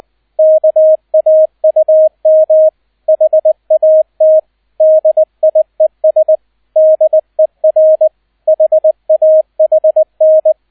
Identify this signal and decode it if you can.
Morse code, heard as a keyed tone, about 20 wpm, 630 Hz, with radio noise. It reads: KAUM HAT DIES DER HAHN
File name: Erster Streich 20wpm